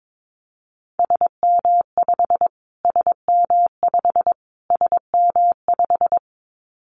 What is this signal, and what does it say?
Morse code, heard as a keyed tone, about 22 wpm, 700 Hz. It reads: SM5 SM5 SM5
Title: Callsign Prefixes for Top 25 DXCC - Repeated 3x - 22wpm